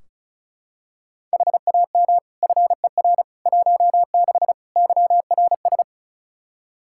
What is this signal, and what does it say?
Morse code, heard as a keyed tone, about 34 wpm, 700 Hz. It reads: HAM FER 16 YRS
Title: Practice QSO - Exchange 3 - More Info - 35wpm